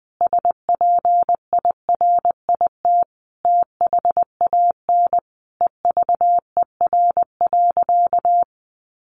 Morse code, heard as a keyed tone, about 20 wpm, 700 Hz. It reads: SPIRIT THAN EVER.
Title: Sentences from Top 1400 Words 20wpm